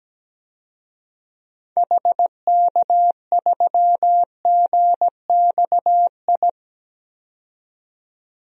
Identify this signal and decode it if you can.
Morse code, heard as a keyed tone, about 17 wpm, 700 Hz. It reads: HK3GXI
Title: DXCC-100 Call Signs - 6 Characters - 17wpm